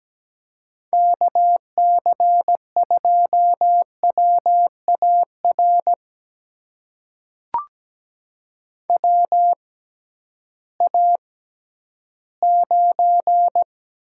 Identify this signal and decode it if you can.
Morse code, heard as a keyed tone, about 17 wpm, spaced as for 11 wpm, 700 Hz. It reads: KC2WAR WA9
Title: USA Call Signs - 6 Characters - 6x Character Spacing - 17wpm